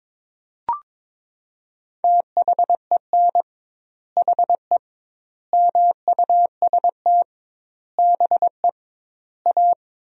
Morse code, heard as a keyed tone, about 22 wpm, 700 Hz. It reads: THEN HE MUST BE A
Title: 2x Word-Spacing - Sentences from Top 500 Words - 22wpm